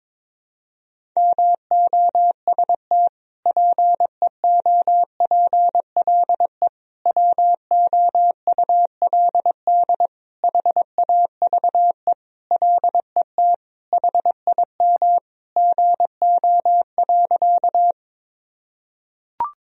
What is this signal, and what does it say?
Morse code, heard as a keyed tone, about 22 wpm, 700 Hz. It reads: MOST PEOPLE WOULD HAVE LET HIM GO.
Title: Sentences from Top 1000 Words - Review Part 1 - 22wpm